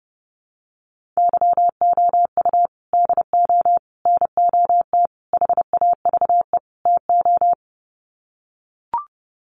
Morse code, heard as a keyed tone, about 30 wpm, 700 Hz. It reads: YOU DO NOT HAVE TO
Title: Sets of 5 Words 30wpm